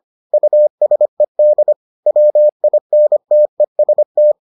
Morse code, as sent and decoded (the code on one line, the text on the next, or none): ..- ... -.. / .-- .. -. - . ... -
USED WINTEST